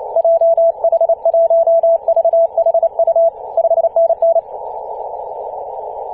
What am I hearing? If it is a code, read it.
Morse code, heard as a keyed tone, about 28 wpm, 670 Hz. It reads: JH1VHU 5NN
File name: ep2a-21-029mhz-cw